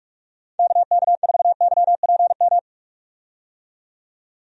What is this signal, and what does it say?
Morse code, heard as a keyed tone, about 41 wpm, 700 Hz. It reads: KK4YPM
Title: USA Call Signs - 6 Characters - 45wpm